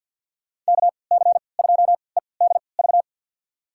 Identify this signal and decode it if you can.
Morse code, heard as a keyed tone, about 53 wpm, spaced as for 29 wpm, 700 Hz. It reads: KC2EDV
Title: USA Call Signs - 6 Characters - 3x Character Spacing - 50wpm